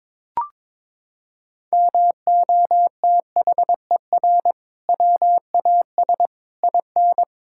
Morse code, heard as a keyed tone, about 22 wpm, 700 Hz. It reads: MOTHER WAS IN
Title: Sentences from Top 1000 Words - Review Part 6 - 22wpm